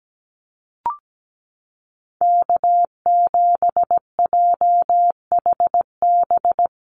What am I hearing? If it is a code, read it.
Morse code, heard as a keyed tone, about 17 wpm, 700 Hz. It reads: K7JHB
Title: USA Call Signs Repeated 2x 17wpm